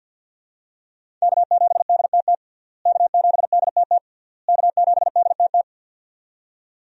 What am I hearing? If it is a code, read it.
Morse code, heard as a keyed tone, about 50 wpm, 700 Hz. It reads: K7DTT K7DTT K7DTT
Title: USA Call Signs - 5 Characters - Repeated 3x - 50wpm